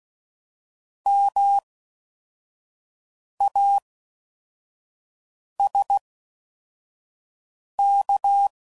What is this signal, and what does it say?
Morse code, heard as a keyed tone, about 16 wpm, spaced as for 4 wpm, 800 Hz. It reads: MASK